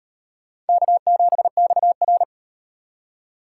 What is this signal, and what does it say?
Morse code, heard as a keyed tone, about 38 wpm, 700 Hz. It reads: K7XR